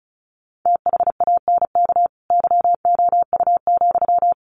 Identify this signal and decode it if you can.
Morse code, heard as a keyed tone, about 34 wpm, 700 Hz. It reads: THANK YOU,
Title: Sentences from Top 1100 Words 35wpm